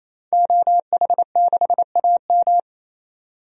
Morse code, heard as a keyed tone, about 27 wpm, 700 Hz. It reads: OH6AM
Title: DXCC-100 Call Signs - 5 Characters - 28wpm